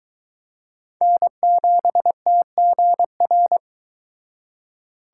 Morse code, heard as a keyed tone, about 22 wpm, 700 Hz. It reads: N7TGR